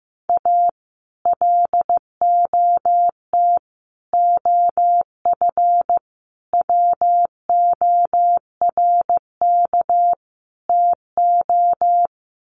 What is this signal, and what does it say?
Morse code, heard as a keyed tone, about 15 wpm, 700 Hz. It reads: A LOT OF WORK TO
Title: Sets of 5 Words Encore2 15wpm